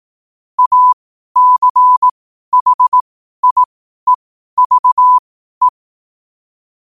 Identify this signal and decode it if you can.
Morse code, heard as a keyed tone, about 18 wpm, spaced as for 12 wpm, 1000 Hz. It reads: ACHIEVE
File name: qotdfiles / 12wpm